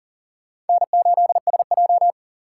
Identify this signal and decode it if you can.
Morse code, heard as a keyed tone, about 40 wpm, 700 Hz. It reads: N8SJ
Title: USA Call Signs - 6x Character Spacing - 40wpm